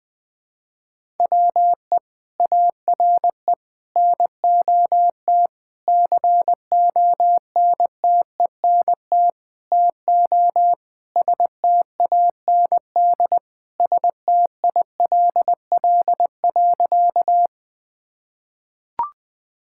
Morse code, heard as a keyed tone, about 20 wpm, 700 Hz. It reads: WE ARE NOT CONTENT TO STAND STILL.
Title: Sentences from Top 1000 Words - Review Part 6 - 20wpm